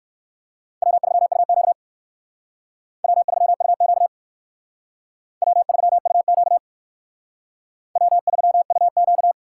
Morse code, heard as a keyed tone, about 57 wpm, 700 Hz. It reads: W3UQ W3UQ W3UQ W3UQ